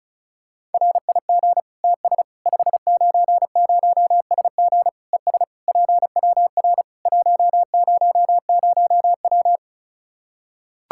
Morse code, heard as a keyed tone, about 34 wpm, 700 Hz. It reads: RIG TS 590SG ES PWR 100W
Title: K1USN SST QSO Practice 35wpm